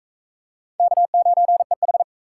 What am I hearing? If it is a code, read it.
Morse code, heard as a keyed tone, about 42 wpm, 700 Hz. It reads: K9EH